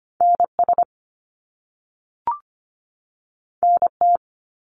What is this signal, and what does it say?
Morse code, heard as a keyed tone, about 25 wpm, 700 Hz. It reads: NS NT
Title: Canadian Provinces and Territories 25wpm